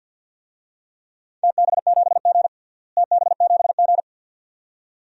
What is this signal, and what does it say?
Morse code, heard as a keyed tone, about 47 wpm, 700 Hz. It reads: TB7G TB7G
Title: DXCC-100 Call Signs - 4 Characters - Repeated 2x - 50wpm